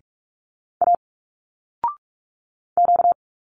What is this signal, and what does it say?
Morse code, heard as a keyed tone, about 46 wpm, 700 Hz. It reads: A Q
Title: Single Letters 45wpm